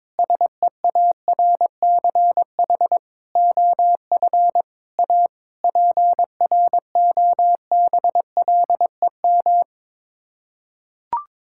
Morse code, heard as a keyed tone, about 22 wpm, 700 Hz. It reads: SEARCH OF A PROBLEM
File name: practice / 22wpm / Sets of 8 Words - 22wpm